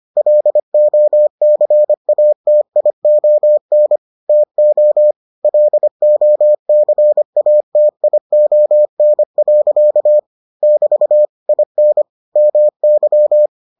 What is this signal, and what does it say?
Morse code, heard as a keyed tone, about 25 wpm, 600 Hz. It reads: LOCATION TO LOCATION. <BT> IN MY